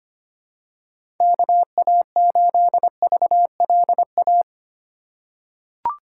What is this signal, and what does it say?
Morse code, heard as a keyed tone, about 25 wpm, 700 Hz. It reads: KA8VLA
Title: USA Call Signs - 6 Characters - 3x Character Spacing - 25wpm